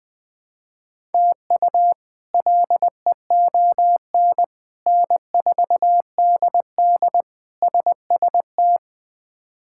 Morse code, heard as a keyed tone, about 20 wpm, 700 Hz. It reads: TU LEON N4DD SST
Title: K1USN SST QSO Practice - 20wpm